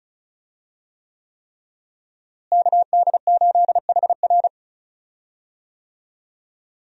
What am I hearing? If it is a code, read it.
Morse code, heard as a keyed tone, about 36 wpm, 700 Hz. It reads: KD8HR
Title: USA Call Signs - 5 Characters - Repeated 3x - 35wpm